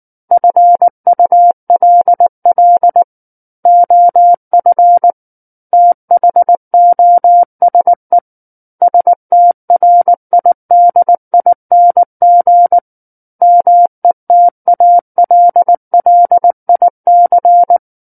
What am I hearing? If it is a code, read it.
Morse code, heard as a keyed tone, about 19 wpm, 700 Hz. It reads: FULL OF THOSE STRIDING METALLIC
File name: Book One / 10-Chapter 10 - 19 WPM